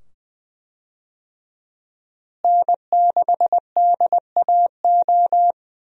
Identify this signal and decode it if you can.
Morse code, heard as a keyed tone, about 20 wpm, 700 Hz. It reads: N6DAO
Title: USA Call Signs - 5 Characters - Repeated 2x - 20wpm